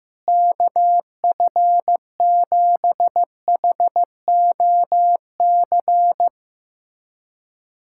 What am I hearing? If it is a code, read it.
Morse code, heard as a keyed tone, about 15 wpm, 700 Hz. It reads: KF7HOC
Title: USA Callsign DE Callsign 15wpm